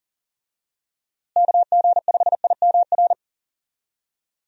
Morse code, heard as a keyed tone, about 40 wpm, 700 Hz. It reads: KG5IMR
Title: USA Call Signs - 6 Characters - Repeated 2x - 40wpm